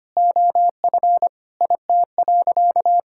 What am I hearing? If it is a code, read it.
Morse code, heard as a keyed tone, about 25 wpm, 700 Hz. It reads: OF IT.
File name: practice / 25wpm / Sentences with 2-Word Phrases - 25wpm